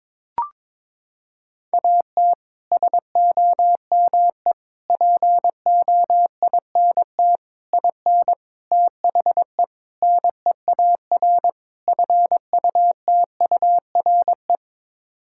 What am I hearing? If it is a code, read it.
Morse code, heard as a keyed tone, about 22 wpm, 700 Hz. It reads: AT SOME POINT IN THE NEAR FUTURE
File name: practice / 22wpm / Sets of 7 Words - 22wpm